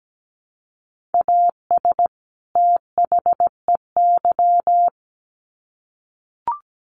Morse code, heard as a keyed tone, about 17 wpm, 700 Hz. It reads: AS THEY
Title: Sets of 2 Words 17wpm